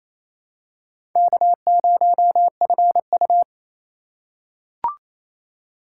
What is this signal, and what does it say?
Morse code, heard as a keyed tone, about 28 wpm, 700 Hz. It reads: K0FU
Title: USA Call Signs - 4x Character Spacing - 28wpm